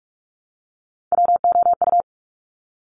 Morse code, heard as a keyed tone, about 46 wpm, 690 Hz. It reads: WOU
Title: Top Three Letter Combinations in Words 45wpm